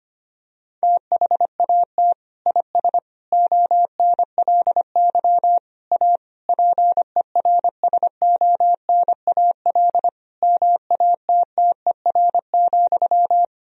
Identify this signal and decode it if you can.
Morse code, heard as a keyed tone, about 25 wpm, 700 Hz. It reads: THAT IS ONLY A PERSONAL MATTER,
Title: Sentences from Top 1000 Words - Review Part 4 - 25wpm